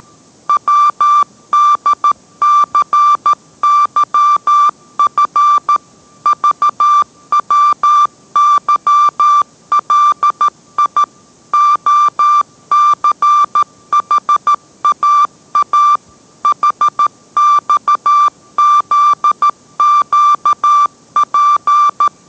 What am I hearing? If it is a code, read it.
Morse code, heard as a keyed tone, about 13 wpm, 1180 Hz. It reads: WDCYF VWYLI OCHAA HXZQP